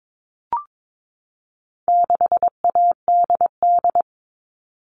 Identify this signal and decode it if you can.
Morse code, heard as a keyed tone, about 22 wpm, 700 Hz. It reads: 6ADD
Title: All Cut Numbers - 4-Digit 22wpm